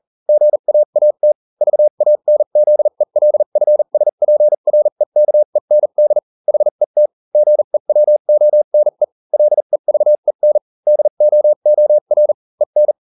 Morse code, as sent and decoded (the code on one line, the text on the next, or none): --. .- .- - / ...- .- -. --.. .-.. ..-. ... .--. .-. -.- . -. -.. / .... - / --. .-- --- -. / .-.. ...- . -. / -.. --- --- .-. / . -.
GAAT VANZELFSPREKEND HET GEWONE LEVEN DOOR EN